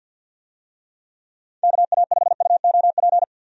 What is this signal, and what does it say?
Morse code, heard as a keyed tone, about 46 wpm, 700 Hz. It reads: KA5UOP
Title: USA Call Signs - 6 Characters - 50wpm